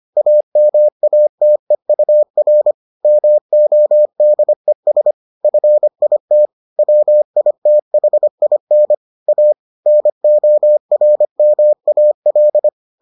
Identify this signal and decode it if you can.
Morse code, heard as a keyed tone, about 25 wpm, 600 Hz. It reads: AMATEUR MODES FIT WITHIN A NORMAL